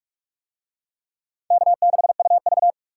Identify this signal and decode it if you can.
Morse code, heard as a keyed tone, about 44 wpm, 700 Hz. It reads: K6UV